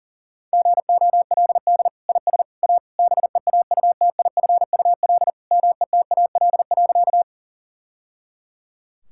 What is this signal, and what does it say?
Morse code, heard as a keyed tone, about 37 wpm, 700 Hz. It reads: GOLD IS A BEAUTIFUL METAL.
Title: Sentences from Top 1000 Words - Review Part 6 - 40wpm